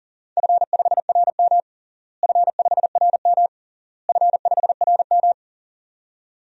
Morse code, heard as a keyed tone, about 40 wpm, 700 Hz. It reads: F5RM F5RM F5RM
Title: DXCC-100 Call Signs - 4 Characters - Repeated 3x - 40wpm